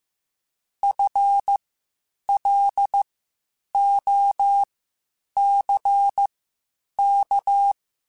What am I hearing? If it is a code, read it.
Morse code, heard as a keyed tone, about 15 wpm, spaced as for 8 wpm, 800 Hz. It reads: FLOCK